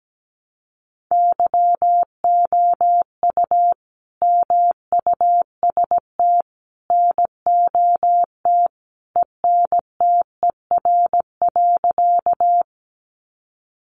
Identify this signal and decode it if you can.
Morse code, heard as a keyed tone, about 17 wpm, 700 Hz. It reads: YOU MUST NOT ENTER.
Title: Sentences from Top 1300 Words 17wpm